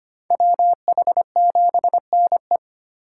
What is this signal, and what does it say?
Morse code, heard as a keyed tone, about 25 wpm, 700 Hz. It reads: WH7NE
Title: USA Call Signs - 5 Characters - 8x Character Spacing - 25wpm